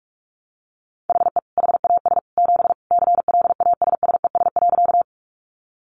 Morse code, heard as a keyed tone, about 44 wpm, 700 Hz. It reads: HE HAS 7 CLASSES.
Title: Additional Easy Phrases 45wpm